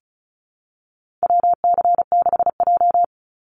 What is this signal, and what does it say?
Morse code, heard as a keyed tone, about 34 wpm, 700 Hz. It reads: WC6J
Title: NA SOTA Chaser Callsigns 35wpm